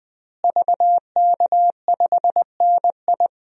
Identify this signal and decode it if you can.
Morse code, heard as a keyed tone, about 21 wpm, 700 Hz. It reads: VK5NI
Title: DXCC-100 Call Signs - 5 Characters - Repeated 2x - 20wpm